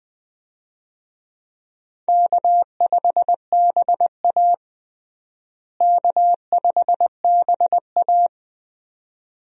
Morse code, heard as a keyed tone, about 20 wpm, 700 Hz. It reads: K5BA K5BA
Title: USA Call Signs - 4 Characters - Repeated 2x - 20wpm